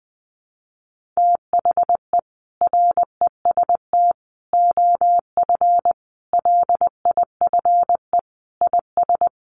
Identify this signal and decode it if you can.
Morse code, heard as a keyed tone, about 20 wpm, 700 Hz. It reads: THE REST OF LIFE IS
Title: Sentences from Top 1500 Words 20wpm